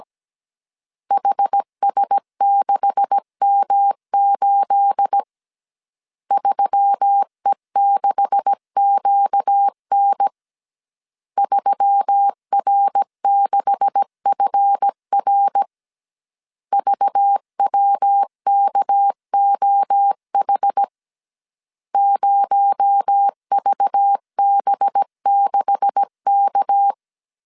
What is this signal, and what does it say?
Morse code, heard as a keyed tone, about 17 wpm, 800 Hz. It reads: HS6M8 3E6QN 3R6FR VWKOH 0VB6K
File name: CW / DET / 1 9 1500C S C 16wpm 2909